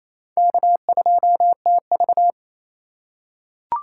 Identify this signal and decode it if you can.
Morse code, heard as a keyed tone, about 27 wpm, 700 Hz. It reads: K2TV
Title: K1USN SST QSO Practice - 28wpm